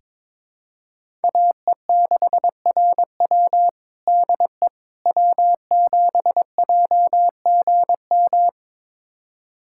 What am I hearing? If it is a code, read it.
Morse code, heard as a keyed tone, about 21 wpm, 700 Hz. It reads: AE6RW DE W7JGM
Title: USA Callsign DE Callsign - 5 Characters - 22wpm